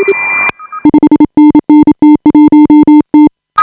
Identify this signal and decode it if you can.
Morse code, heard as a keyed tone, about 32 wpm, 310 Hz. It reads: 5NN T1T
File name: AUDIO2018 / HA5BA CW